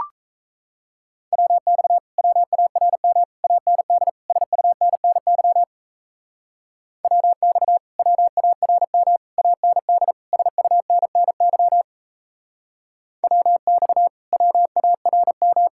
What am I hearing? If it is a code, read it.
Morse code, heard as a keyed tone, about 37 wpm, 700 Hz. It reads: WX WARM AND SUNNY WX WARM AND SUNNY WX WARM